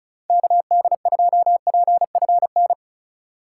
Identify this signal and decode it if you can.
Morse code, heard as a keyed tone, about 34 wpm, 700 Hz. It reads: KD2PFN
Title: USA Call Signs - 6 Characters - 4x Character Spacing - 35wpm